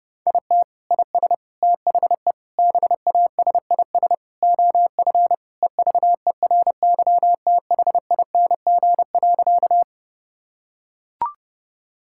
Morse code, heard as a keyed tone, about 30 wpm, 700 Hz. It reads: IT IS THE BASIS OF EVERYTHING.
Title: Sentences from Top 1000 Words - Review Part 5 - 30wpm